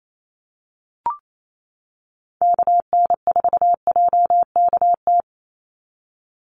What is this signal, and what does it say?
Morse code, heard as a keyed tone, about 28 wpm, 700 Hz. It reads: KN4JKT
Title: USA Call Signs 28wpm